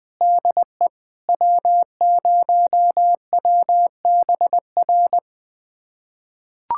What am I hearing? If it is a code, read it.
Morse code, heard as a keyed tone, about 20 wpm, 700 Hz. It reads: DE W0WBR
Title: USA Callsign DE Callsign - 5 Characters - 20wpm